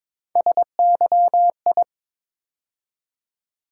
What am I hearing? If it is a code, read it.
Morse code, heard as a keyed tone, about 22 wpm, 700 Hz. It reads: SYI
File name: practice / 22wpm / Three Letter-Number Combinations - 22wpm